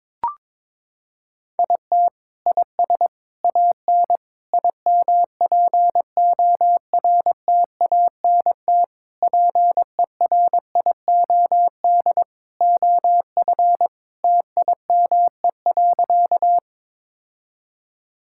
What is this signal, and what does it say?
Morse code, heard as a keyed tone, about 22 wpm, 700 Hz. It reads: IT IS AN IMPORTANT PERIOD OF TIME.
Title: Sentences from Top 1000 Words - Review Part 5 - 22wpm